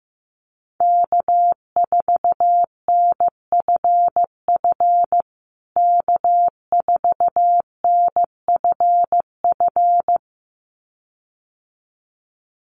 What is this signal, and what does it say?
Morse code, heard as a keyed tone, about 15 wpm, 700 Hz. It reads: K4NFF K4NFF
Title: USA Call Signs Repeated 2x 15wpm